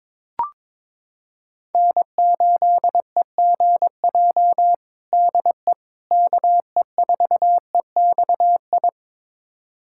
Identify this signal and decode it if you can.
Morse code, heard as a keyed tone, about 22 wpm, 700 Hz. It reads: N8EGJ DE KE4EXI
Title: USA Callsign DE Callsign 22wpm